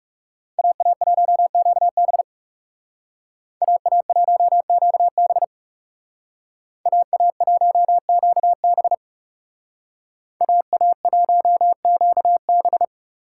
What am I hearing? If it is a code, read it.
Morse code, heard as a keyed tone, about 38 wpm, 700 Hz. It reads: AA1QB AA1QB AA1QB AA1QB